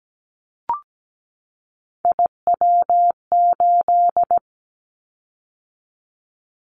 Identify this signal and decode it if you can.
Morse code, heard as a keyed tone, about 17 wpm, 700 Hz. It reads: IW8
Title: Callsign Prefixes for Top 25 DXCC - 17wpm